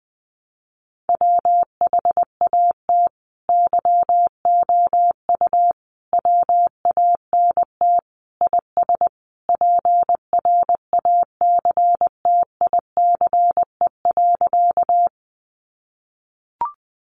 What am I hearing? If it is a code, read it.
Morse code, heard as a keyed tone, about 20 wpm, 700 Hz. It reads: WHAT YOU WANT IS PRACTICE.
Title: Sentences from Top 700 Words 20wpm